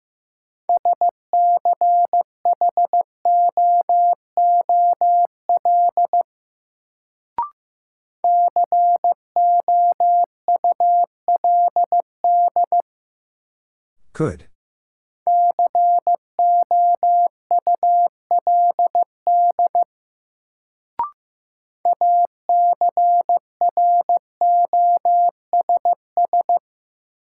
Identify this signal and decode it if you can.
Morse code, heard as a keyed tone, about 15 wpm, 700 Hz. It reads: SCHOOL COULD COULD ACROSS
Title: Sets of 2 Words - New Word Drill - 15wpm